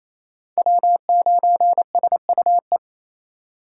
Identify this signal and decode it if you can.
Morse code, heard as a keyed tone, about 35 wpm, 700 Hz. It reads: W9SUE
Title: Speed Racing - USA 1x3 Call Signs 28wpm